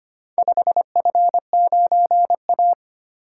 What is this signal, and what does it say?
Morse code, heard as a keyed tone, about 25 wpm, 700 Hz. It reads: 5F9A
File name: practice / 25wpm / Four Letter-Number Combinations - 25wpm